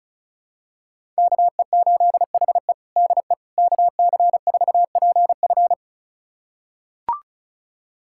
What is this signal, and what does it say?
Morse code, heard as a keyed tone, about 36 wpm, 700 Hz. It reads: KE8HE DE KC4PF
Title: USA Callsign DE Callsign - 5 Characters - 35wpm